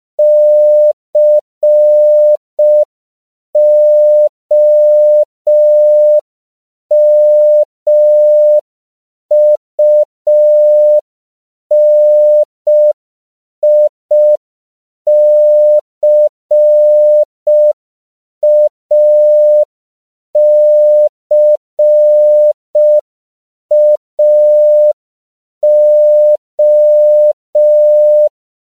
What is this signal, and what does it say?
Morse code, heard as a keyed tone, about 5 wpm, 600 Hz. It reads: COMUNICACAO